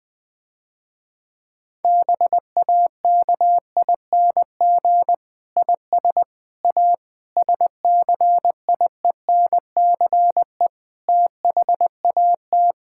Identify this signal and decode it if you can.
Morse code, heard as a keyed tone, about 20 wpm, 700 Hz. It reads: BAKING IS A SCIENCE THAT
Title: Difficult Sentences with Hints - 20wpm